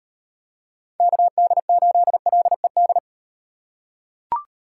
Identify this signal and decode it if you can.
Morse code, heard as a keyed tone, about 38 wpm, 700 Hz. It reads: KD8LED